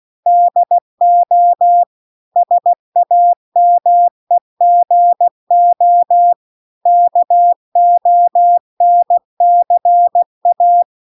Morse code, heard as a keyed tone, about 16 wpm, 700 Hz. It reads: DO SAMEGO KONCA